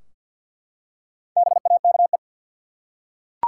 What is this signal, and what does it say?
Morse code, heard as a keyed tone, about 50 wpm, 700 Hz. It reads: BAKE
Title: Four Letter Words - 50wpm